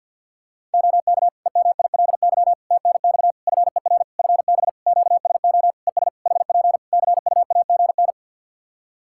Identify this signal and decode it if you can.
Morse code, heard as a keyed tone, about 52 wpm, 700 Hz. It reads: OK EMILY TNX FER FB QSO ES HP CUAGN
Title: Practice QSO - Exchange 4 - Ending - 50wpm